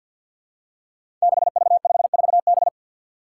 Morse code, heard as a keyed tone, about 52 wpm, 700 Hz. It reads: 6V54B
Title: Five Challenging Letter-Number Combinations 50wpm